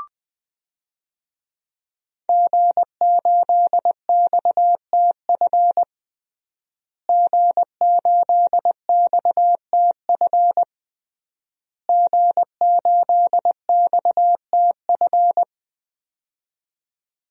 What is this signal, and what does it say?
Morse code, heard as a keyed tone, about 20 wpm, 700 Hz. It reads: G8XTF G8XTF G8XTF
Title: DXCC-100 Call Signs - 5 Characters - Repeated 3x - 20wpm